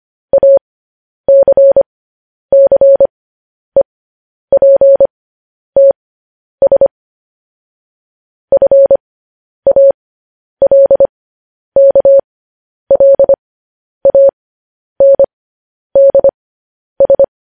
Morse code, heard as a keyed tone, about 25 wpm, spaced as for 10 wpm, 550 Hz. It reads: ACCEPTS FALKLANDS
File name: podcast / News headlines-1x-25x10